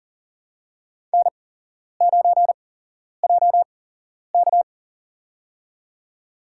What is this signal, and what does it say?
Morse code, heard as a keyed tone, about 40 wpm, spaced as for 11 wpm, 700 Hz. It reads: N9JK
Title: USA Call Signs - 8x Character Spacing - 40wpm